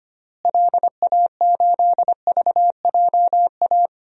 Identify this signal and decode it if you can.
Morse code, heard as a keyed tone, about 25 wpm, 700 Hz. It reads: LA8VJA
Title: DXCC-100 Call Signs - 6 Characters - Repeated 3x - 25wpm